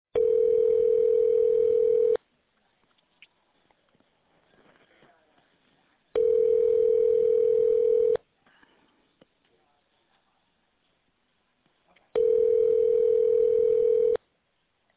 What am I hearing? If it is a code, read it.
Ringback tone (North America), 440 + 480 Hz.